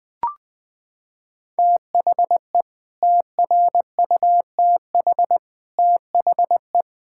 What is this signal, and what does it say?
Morse code, heard as a keyed tone, about 21 wpm, 700 Hz. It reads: THE TRUTH THE
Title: Sets of 9 Words - 20wpm